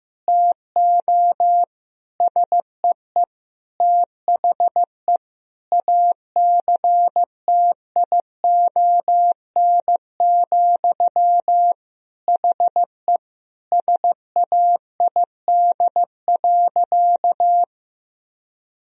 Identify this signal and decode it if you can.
Morse code, heard as a keyed tone, about 15 wpm, 700 Hz. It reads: TO SEE THE ACTION, HE SAID.
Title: Sentences from Top 1000 Words - Review Part 4 - 15wpm